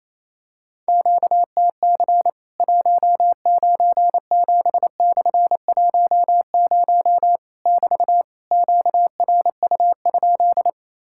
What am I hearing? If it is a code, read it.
Morse code, heard as a keyed tone, about 28 wpm, 700 Hz. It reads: QTC 197/10 <BT> QRU?
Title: Worked All Europe DX Contest - QTC Exchange - 28wpm